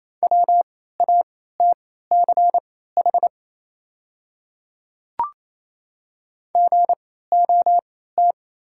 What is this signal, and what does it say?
Morse code, heard as a keyed tone, about 28 wpm, spaced as for 16 wpm, 690 Hz. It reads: WATCH GOT
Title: 3x Character Spacing - Top 300 Words - 28wpm